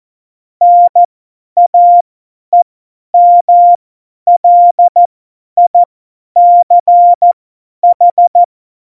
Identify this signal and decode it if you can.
Morse code, heard as a keyed tone, about 14 wpm, spaced as for 10 wpm, 700 Hz. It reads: NAEMLICH
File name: Morsekurs / Witze / Witz 12. Dezember 2017, 50 F70